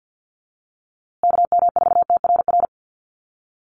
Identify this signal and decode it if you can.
Morse code, heard as a keyed tone, about 50 wpm, 700 Hz. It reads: KM4TRR
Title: USA Call Signs Repeated 2x 50wpm